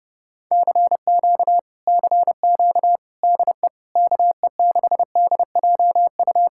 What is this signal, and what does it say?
Morse code, heard as a keyed tone, about 31 wpm, 700 Hz. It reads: CQ CQ DE KE6DJU